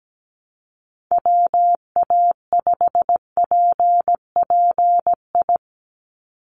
Morse code, heard as a keyed tone, about 17 wpm, 700 Hz. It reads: WA5PPI
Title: USA Call Signs 17wpm